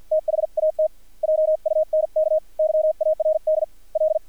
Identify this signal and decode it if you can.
Morse code, heard as a keyed tone, about 50 wpm, spaced as for 44 wpm, 630 Hz, with radio noise. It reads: THAT JUNK YARD L
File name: archived / CRSnet